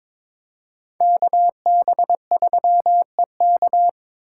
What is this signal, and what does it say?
Morse code, heard as a keyed tone, about 26 wpm, 700 Hz. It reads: KB3EK
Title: Speed Racing - USA 2x2 Call Signs 22wpm